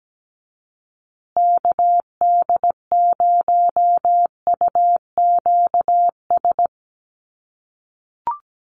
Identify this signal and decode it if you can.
Morse code, heard as a keyed tone, about 17 wpm, 700 Hz. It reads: KD0UQS
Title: USA Call Signs Repeated 3x 17wpm